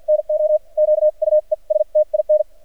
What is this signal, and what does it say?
Morse code, heard as a keyed tone, about 48 wpm, spaced as for 39 wpm, 610 Hz. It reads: NO QUESTIN
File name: archived / CRSnet